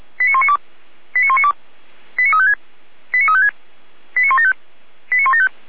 Five-tone selective call (ZVEI), heard as quick runs of five tones, tones of about 80 ms. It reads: 89192 89192 89356 89356 89175 89175